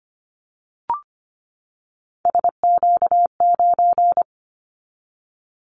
Morse code, heard as a keyed tone, about 25 wpm, 700 Hz. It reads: SQ9
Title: Callsign Prefixes for Top 25 DXCC - 25wpm